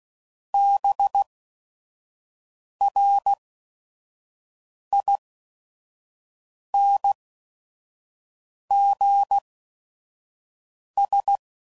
Morse code, heard as a keyed tone, about 16 wpm, spaced as for 5 wpm, 800 Hz. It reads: BRINGS
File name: Sayings / Quotes / Persperation